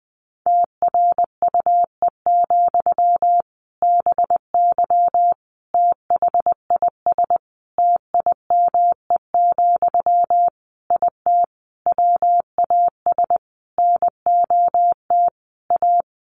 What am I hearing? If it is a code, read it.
Morse code, heard as a keyed tone, about 20 wpm, 700 Hz. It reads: TRUE, BY THIS TIME, IT WAS NOT A
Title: Sentences from Top 1500 Words 20wpm